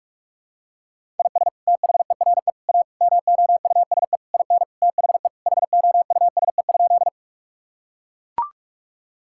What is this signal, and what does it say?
Morse code, heard as a keyed tone, about 41 wpm, 700 Hz. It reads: IS THERE A MOUSE IN THE HOUSE?
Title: Sentences from Top 1000 Words - Review Part 5 - 45wpm